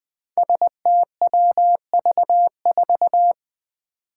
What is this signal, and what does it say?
Morse code, heard as a keyed tone, about 20 wpm, 700 Hz. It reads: STWV4
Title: Five Challenging Letter-Number Combinations 20wpm